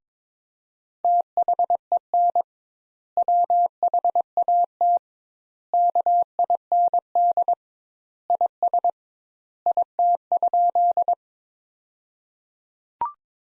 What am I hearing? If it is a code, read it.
Morse code, heard as a keyed tone, about 22 wpm, 700 Hz. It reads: THEN WHAT KIND IS IT?
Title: Sentences Top 200 Words - 22wpm